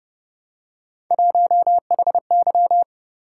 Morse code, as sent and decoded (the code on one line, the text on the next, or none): .---- .... -.--
1HY